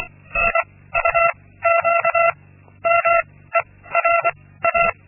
Morse code, heard as a keyed tone, about 24 wpm, 680 Hz. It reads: I U Q M E R A
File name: Linksmakalnis-2